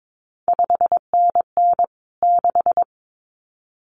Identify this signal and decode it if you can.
Morse code, heard as a keyed tone, about 22 wpm, 700 Hz. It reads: 5NN 6
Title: CQ WW DX Contest Exchange Encore - 22wpm